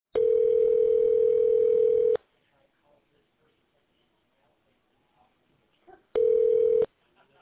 Ringback tone (North America), 440 + 480 Hz.